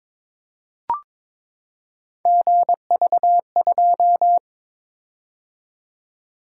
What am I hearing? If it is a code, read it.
Morse code, heard as a keyed tone, about 22 wpm, 700 Hz. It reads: GV2
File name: practice / 22wpm / Three Letter-Number Combinations - 22wpm